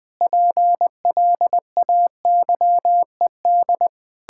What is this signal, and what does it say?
Morse code, heard as a keyed tone, about 19 wpm, 700 Hz. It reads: PLAYED
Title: Difficult Sentences with Hints - 20wpm